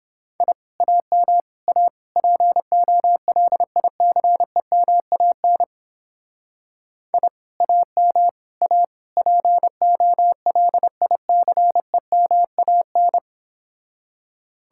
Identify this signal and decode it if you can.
Morse code, heard as a keyed tone, about 31 wpm, 700 Hz. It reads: I AM A POLICEMAN I AM A POLICEMAN